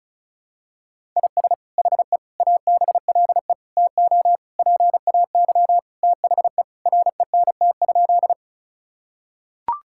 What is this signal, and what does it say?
Morse code, heard as a keyed tone, about 36 wpm, 700 Hz. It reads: IS HE ABLE TO PAY THE RENT?
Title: Sentences from Top 1000 Words - Review Part 5 - 35wpm